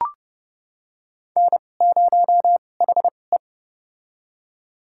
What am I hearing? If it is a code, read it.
Morse code, heard as a keyed tone, about 30 wpm, spaced as for 22 wpm, 700 Hz. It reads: N0HE
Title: USA Call Signs - 2x Character Spacing - 30wpm